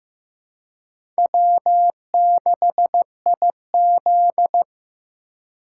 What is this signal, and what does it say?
Morse code, heard as a keyed tone, about 15 wpm, 700 Hz. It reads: W6IZ
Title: USA Call Signs - 4 Characters - 3x Character Spacing - 15wpm